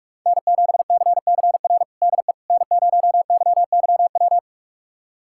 Morse code, heard as a keyed tone, about 41 wpm, 700 Hz. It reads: N7CCR DE N0YYW
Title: USA Callsign DE Callsign - 5 Characters - 45wpm